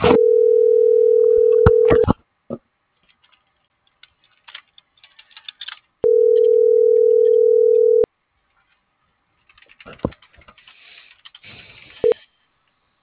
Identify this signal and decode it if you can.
Ringback tone (North America), 440 + 480 Hz.